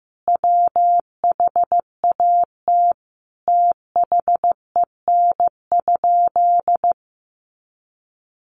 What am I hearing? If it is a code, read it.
Morse code, heard as a keyed tone, about 15 wpm, 700 Hz. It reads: WHAT THEN?
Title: Sentences from Top 1500 Words 15wpm